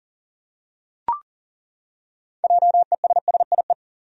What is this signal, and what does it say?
Morse code, heard as a keyed tone, about 40 wpm, 700 Hz. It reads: JESSIE
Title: USA Common Names - 40wpm